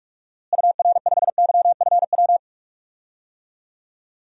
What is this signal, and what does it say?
Morse code, heard as a keyed tone, about 46 wpm, 700 Hz. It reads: UR5YFW
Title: DXCC-100 Call Signs - 6 Characters - Repeated 2x - 45wpm